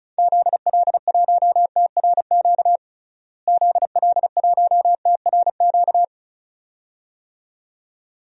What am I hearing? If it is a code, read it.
Morse code, heard as a keyed tone, about 36 wpm, 700 Hz. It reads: ZL1TRQ ZL1TRQ
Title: DXCC-100 Call Signs - 6 Characters - Repeated 2x - 35wpm